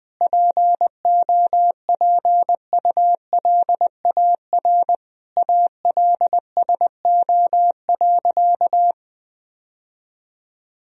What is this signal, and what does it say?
Morse code, heard as a keyed tone, about 21 wpm, 700 Hz. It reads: POPULAR ALSO.
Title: Sentences from Top 1000 Words - Review Part 3 - 20wpm